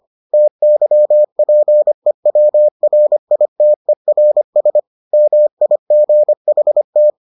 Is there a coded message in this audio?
Yes: Morse code from 0.3 to 7.1 s, heard as a keyed tone, about 25 wpm, 600 Hz. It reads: TYPEWRITERS MIGHT